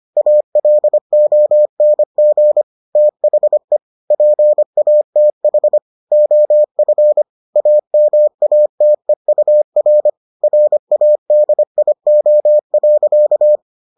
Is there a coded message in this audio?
Yes: Morse code from 0.2 to 13.6 s, heard as a keyed tone, about 25 wpm, 600 Hz. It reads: ALONG THE PATH OF AMATEUR RADIO.